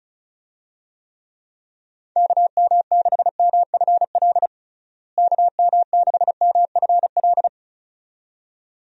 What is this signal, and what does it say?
Morse code, heard as a keyed tone, about 36 wpm, 700 Hz. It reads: KM6MFL KM6MFL
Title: USA Call Signs - 6 Characters - Repeated 2x - 35wpm